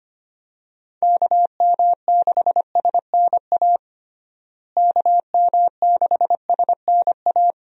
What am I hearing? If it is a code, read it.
Morse code, heard as a keyed tone, about 25 wpm, 700 Hz. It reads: KM6SNA KM6SNA
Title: USA Call Signs - 6 Characters - Repeated 3x - 25wpm